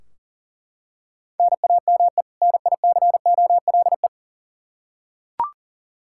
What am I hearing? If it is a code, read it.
Morse code, heard as a keyed tone, about 38 wpm, 700 Hz. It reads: NAME NICOLE
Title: Practice QSO - Exchange 1 - Beginning - 40wpm